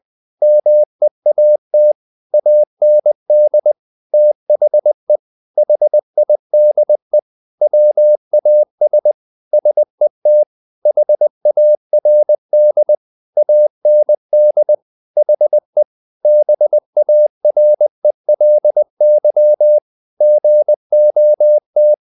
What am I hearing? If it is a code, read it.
Morse code, heard as a keyed tone, about 20 wpm, 600 Hz. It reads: MEAT AND THE HIDE WAS SET HARD AND HE BARELY GOT
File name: the old man and the sea part 2 20wpm